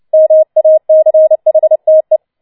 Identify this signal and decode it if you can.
Morse code, heard as a keyed tone, about 29 wpm, 630 Hz. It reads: MACHTE